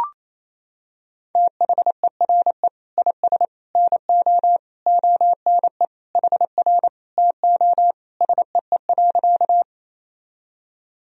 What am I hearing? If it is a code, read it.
Morse code, heard as a keyed tone, about 28 wpm, 700 Hz. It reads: THERE IS NO ONE HR TO SEE.
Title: Sentences from Top 200 Words and Abbreviations 28wpm